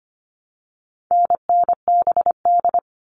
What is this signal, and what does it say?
Morse code, heard as a keyed tone, about 25 wpm, 700 Hz. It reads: NNBD
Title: All Cut Numbers - 4-Digit 25wpm